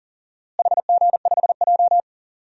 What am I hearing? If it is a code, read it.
Morse code, heard as a keyed tone, about 40 wpm, 700 Hz. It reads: HG5J
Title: DXCC-100 Call Signs - 4 Characters - Repeated 3x - 40wpm